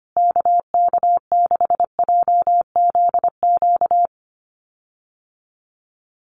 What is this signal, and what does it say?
Morse code, heard as a keyed tone, about 25 wpm, 700 Hz. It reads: KK6JZQ
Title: USA Call Signs Repeated 2x 25wpm